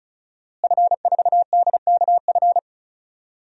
Morse code, heard as a keyed tone, about 34 wpm, 700 Hz. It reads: F4DKF
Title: DXCC-100 Call Signs - 5 Characters - Repeated 2x - 35wpm